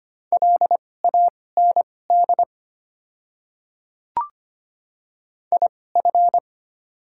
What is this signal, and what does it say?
Morse code, heard as a keyed tone, about 25 wpm, spaced as for 18 wpm, 690 Hz. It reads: LAND IF
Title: 2x Character Spacing - Top 200 Words - 25wpm